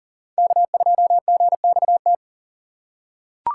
Morse code, heard as a keyed tone, about 40 wpm, 700 Hz. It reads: K2GXT
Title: USA Call Signs - 2x Character Spacing - 40wpm